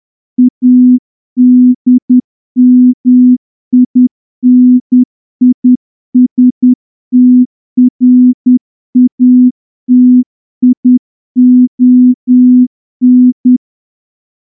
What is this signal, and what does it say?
Morse code, heard as a keyed tone, about 10 wpm, 250 Hz. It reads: ADMINISTRATION
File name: morse-training-10WPM